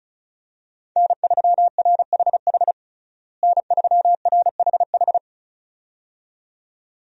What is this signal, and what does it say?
Morse code, heard as a keyed tone, about 36 wpm, 700 Hz. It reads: N3RHH N3RHH
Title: USA Call Signs - 5 Characters - Repeated 2x - 35wpm